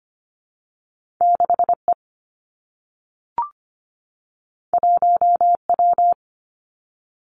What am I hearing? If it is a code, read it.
Morse code, heard as a keyed tone, about 25 wpm, 700 Hz. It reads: 6E 1W
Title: Two Challenging Letter-Number Combinations 25wpm